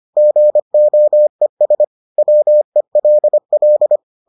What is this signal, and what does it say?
Morse code, heard as a keyed tone, about 25 wpm, 600 Hz. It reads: GOES WELL